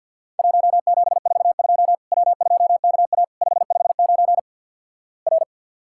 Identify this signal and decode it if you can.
Morse code, heard as a keyed tone, about 50 wpm, 700 Hz. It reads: 1742 W2KA 559 R
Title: Worked All Europe DX Contest - QTC Exchange - 50wpm